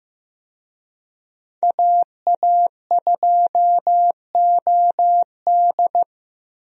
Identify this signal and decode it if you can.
Morse code, heard as a keyed tone, about 15 wpm, 700 Hz. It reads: AA2OD
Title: USA Call Signs - 5 Characters - Repeated 3x - 15wpm